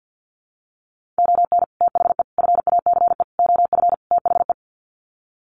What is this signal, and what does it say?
Morse code, heard as a keyed tone, about 48 wpm, 700 Hz. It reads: ON THE FACE OF THE
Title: Sets of 5 Words Encore2 50wpm